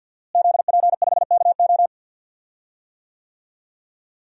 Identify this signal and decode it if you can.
Morse code, heard as a keyed tone, about 52 wpm, 700 Hz. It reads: ZP5KO
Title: DXCC-100 Call Signs - 5 Characters - Repeated 3x - 50wpm